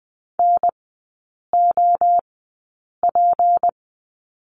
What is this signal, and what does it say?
Morse code, heard as a keyed tone, about 20 wpm, spaced as for 8 wpm, 700 Hz. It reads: NOP
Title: 198 Lesson 32e - Letter k - Three Characters 20wpm